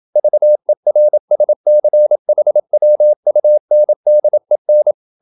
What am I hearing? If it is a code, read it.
Morse code, heard as a keyed tone, about 28 wpm, 600 Hz. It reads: VERSCHWUNDEN